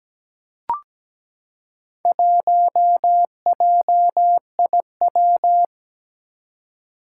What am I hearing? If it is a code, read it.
Morse code, heard as a keyed tone, about 17 wpm, 700 Hz. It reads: 1JIW
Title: Four Challenging Letter-Number Combinations 17wpm